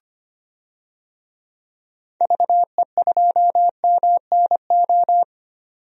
Morse code, heard as a keyed tone, about 25 wpm, 700 Hz. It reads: VE2MNO
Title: DXCC-100 Call Signs - 6 Characters - 25wpm